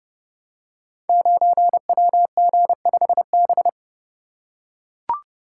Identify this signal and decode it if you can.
Morse code, heard as a keyed tone, about 30 wpm, 700 Hz. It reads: 9WG5B
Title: Code Groups - 5 Letter:Numbers - 30wpm